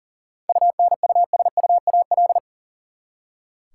Morse code, heard as a keyed tone, about 40 wpm, 700 Hz. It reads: UNUSUAL
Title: Seven Letter Words - 40wpm